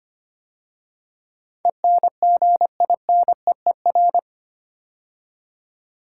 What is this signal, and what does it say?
Morse code, heard as a keyed tone, about 25 wpm, 700 Hz. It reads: ENGINEER
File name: 8 Letter Words - First 3 Letters Given - 25wpm